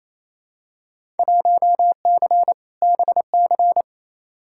Morse code, heard as a keyed tone, about 28 wpm, 700 Hz. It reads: 1C BC
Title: Field Day Exchanges Encore - 28wpm